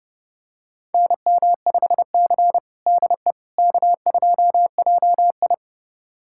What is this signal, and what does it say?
Morse code, heard as a keyed tone, about 30 wpm, 700 Hz. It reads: NM5C DE K2JI
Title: USA Callsign DE Callsign - 4 Characters - 30wpm